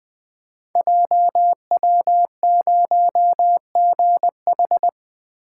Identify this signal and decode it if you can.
Morse code, heard as a keyed tone, about 20 wpm, 700 Hz. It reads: JW0GH
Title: Code Groups - 5 Letter:Numbers - 20wpm